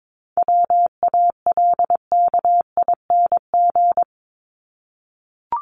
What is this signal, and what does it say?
Morse code, heard as a keyed tone, about 22 wpm, 700 Hz. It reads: WALKING
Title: ING Suffix - Top 100 Words with Base Words 22wpm